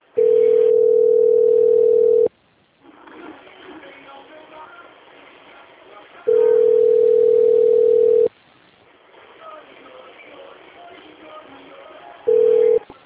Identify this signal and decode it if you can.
Ringback tone (North America), 440 + 480 Hz.